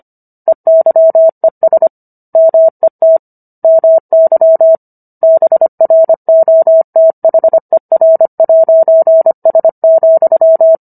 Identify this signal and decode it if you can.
Morse code, heard as a keyed tone, about 25 wpm, 650 Hz. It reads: EYES MET MY BROTHER'S,